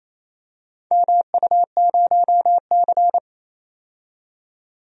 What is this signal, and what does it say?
Morse code, heard as a keyed tone, about 28 wpm, 700 Hz. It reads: MU0C
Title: DXCC-100 Call Signs - 4 Characters - 28wpm